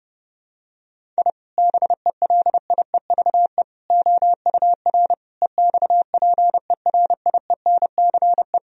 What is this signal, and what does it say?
Morse code, heard as a keyed tone, about 30 wpm, 700 Hz. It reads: I BELIEVE OUR EXPERIENCE
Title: Sentences from Top 1000 Words - Review Part 3 - 30wpm